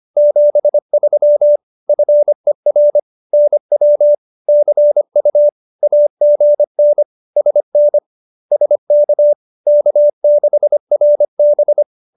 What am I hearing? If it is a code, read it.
Morse code, heard as a keyed tone, about 25 wpm, 590 Hz. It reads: 73 FER NW CU AGN SN SK K6RB